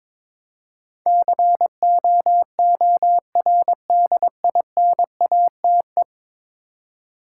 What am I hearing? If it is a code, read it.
Morse code, heard as a keyed tone, about 22 wpm, 700 Hz. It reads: COORDINATE
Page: morse-code-ninja-practice-22wpm